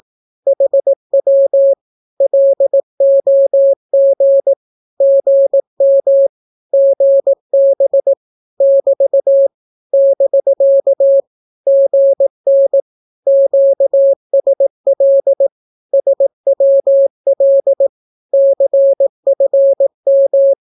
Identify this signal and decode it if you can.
Morse code, heard as a keyed tone, about 18 wpm, 550 Hz. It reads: HW LOG GM GB <BT> <BK> GN QSL SWL CFM
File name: naukapodstaw / skroty / skroty